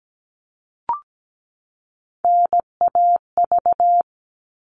Morse code, heard as a keyed tone, about 17 wpm, 700 Hz. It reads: NAV
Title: All Cut Numbers - 3-Digit 17wpm